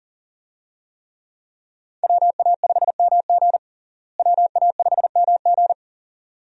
Morse code, heard as a keyed tone, about 40 wpm, 700 Hz. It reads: WA5MG WA5MG
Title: USA Call Signs - 5 Characters - Repeated 2x - 40wpm